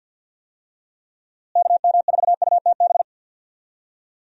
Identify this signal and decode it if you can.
Morse code, heard as a keyed tone, about 47 wpm, 700 Hz. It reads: KM4UTB
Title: USA Call Signs - 6 Characters - Repeated 2x - 50wpm